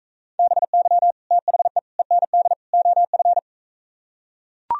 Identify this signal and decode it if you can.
Morse code, heard as a keyed tone, about 43 wpm, 700 Hz. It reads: BY THE END OF